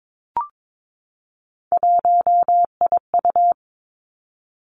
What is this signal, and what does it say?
Morse code, heard as a keyed tone, about 22 wpm, 700 Hz. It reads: 1IU
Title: Three Challenging Letter-Number Combinations 22wpm